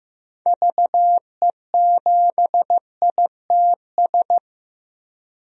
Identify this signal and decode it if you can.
Morse code, heard as a keyed tone, about 15 wpm, 700 Hz. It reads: VE7ITS
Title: DXCC-100 Call Signs - 6 Characters - 15wpm